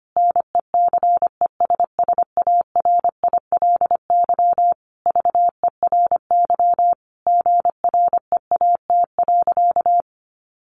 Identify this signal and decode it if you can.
Morse code, heard as a keyed tone, about 25 wpm, 700 Hz. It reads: NECESSARILY VERY GREAT.
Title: Sentences from Top 1400 Words 25wpm